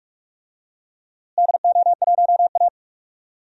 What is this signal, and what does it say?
Morse code, heard as a keyed tone, about 46 wpm, 700 Hz. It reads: DO1A
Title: DXCC-100 Call Signs - 4 Characters - Repeated 3x - 45wpm